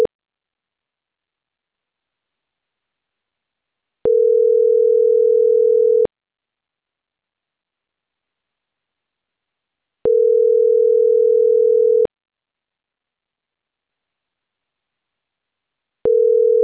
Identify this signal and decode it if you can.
Ringback tone (North America), 440 + 480 Hz.